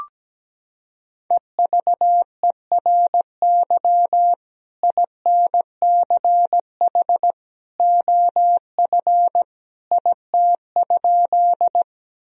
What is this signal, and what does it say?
Morse code, heard as a keyed tone, about 17 wpm, 700 Hz. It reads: EVERY INCH OF IT?
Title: Sentences from Top 1000 Words - Review Part 3 - 17wpm